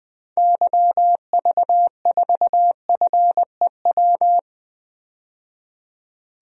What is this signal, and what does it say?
Morse code, heard as a keyed tone, about 20 wpm, 700 Hz. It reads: YV4FEW
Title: DXCC-100 Call Signs - 6 Characters - 20wpm